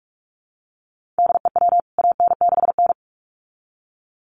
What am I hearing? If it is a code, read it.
Morse code, heard as a keyed tone, about 44 wpm, 700 Hz. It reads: DEW AN6N
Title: CWT Exchange 45wpm